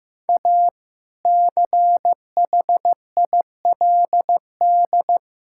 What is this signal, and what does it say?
Morse code, heard as a keyed tone, about 15 wpm, 700 Hz. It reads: A CHILD
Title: Sentences from Top 1000 Words - Review Part 6 - 15wpm